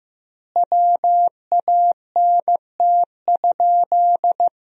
Morse code, heard as a keyed tone, about 15 wpm, 700 Hz. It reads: WANT?
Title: 2x Word-Spacing - Sentences from Top 400 Words - 15wpm